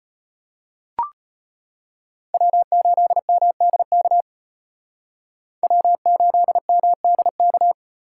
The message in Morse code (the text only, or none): W8MDK W8MDK